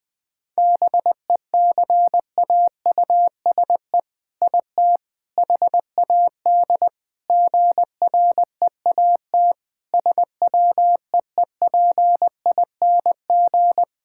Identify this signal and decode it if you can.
Morse code, heard as a keyed tone, about 20 wpm, 700 Hz. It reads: BECAUSE IT HAD GREAT SWEEPING
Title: All-Age Appropriate Jokes - 20wpm